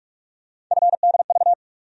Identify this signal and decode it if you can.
Morse code, heard as a keyed tone, about 46 wpm, 700 Hz. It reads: FDV
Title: USA Call Sign Postfixes - 3 Letters 45wpm